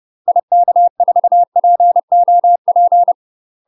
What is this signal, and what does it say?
Morse code, heard as a keyed tone, about 30 wpm, 700 Hz. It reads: IK4POP
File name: UK MASTER QSOs TRAINING / 30WPM / UK MASTER QSOs 0017